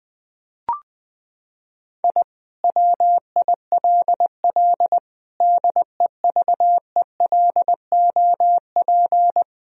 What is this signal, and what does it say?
Morse code, heard as a keyed tone, about 21 wpm, 700 Hz. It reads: I WILL DEVELOP
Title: Sentences from Top 1000 Words - Review Part 2 - 20wpm